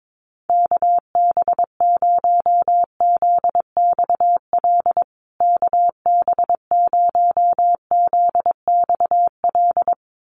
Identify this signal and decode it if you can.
Morse code, heard as a keyed tone, about 22 wpm, 700 Hz. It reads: KB0ZXL KB0ZXL
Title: USA Call Signs Repeated 2x 22wpm